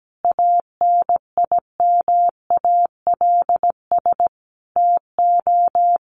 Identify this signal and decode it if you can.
Morse code, heard as a keyed tone, about 17 wpm, 700 Hz. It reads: ANIMALS TO
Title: A Sound Of Thunder 17wpm